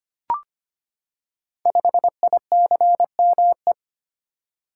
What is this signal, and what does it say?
Morse code, heard as a keyed tone, about 25 wpm, 700 Hz. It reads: 5ICME
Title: Code Groups - 5 Letter:Numbers - 25wpm